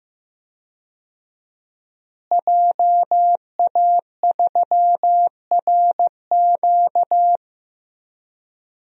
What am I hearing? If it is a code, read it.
Morse code, heard as a keyed tone, about 15 wpm, 700 Hz. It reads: JA3RQ
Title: DXCC-100 Call Signs - 5 Characters - Repeated 3x - 15wpm